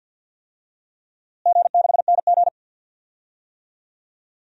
Morse code, heard as a keyed tone, about 50 wpm, 700 Hz. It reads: G6NG
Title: DXCC-100 Call Signs - 4 Characters - Repeated 2x - 50wpm